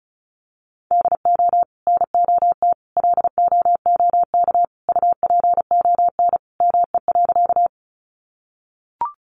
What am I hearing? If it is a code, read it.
Morse code, heard as a keyed tone, about 34 wpm, 700 Hz. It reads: DO NOT LOOK UPON ME.
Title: Sentences from Top 1100 Words 35wpm